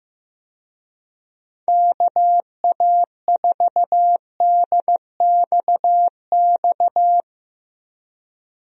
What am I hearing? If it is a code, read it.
Morse code, heard as a keyed tone, about 15 wpm, 700 Hz. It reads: KA4DXX
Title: USA Call Signs - 6 Characters - Repeated 3x - 15wpm